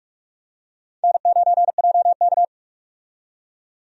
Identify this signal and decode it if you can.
Morse code, heard as a keyed tone, about 44 wpm, 700 Hz. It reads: N9JK
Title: USA Call Signs - 4 Characters - 45wpm